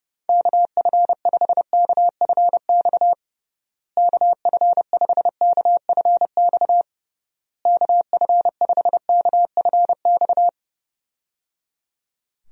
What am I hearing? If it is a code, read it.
Morse code, heard as a keyed tone, about 32 wpm, 700 Hz. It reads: KF5KFX KF5KFX KF5KFX
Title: USA Call Signs - 6 Characters - Repeated 3x - 30wpm